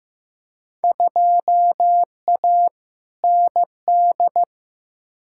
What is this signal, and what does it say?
Morse code, heard as a keyed tone, about 15 wpm, 700 Hz. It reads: 2A ND
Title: Field Day Exchanges Encore - 15wpm